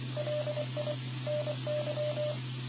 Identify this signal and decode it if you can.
Morse code, heard as a keyed tone, about 25 wpm, 600 Hz, with noise. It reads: LI NY